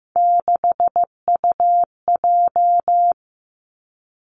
Morse code, heard as a keyed tone, about 15 wpm, 700 Hz. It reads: 6UJ
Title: Three Challenging Letter-Number Combinations 15wpm